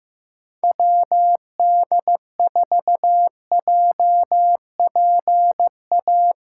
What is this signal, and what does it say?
Morse code, heard as a keyed tone, about 15 wpm, 700 Hz. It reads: WD4JPA
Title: USA Callsign DE Callsign 15wpm